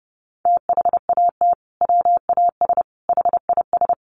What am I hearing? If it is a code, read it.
Morse code, heard as a keyed tone, about 30 wpm, 700 Hz. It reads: THAT WAS HIS
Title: Sentences from Top 1500 Words 30wpm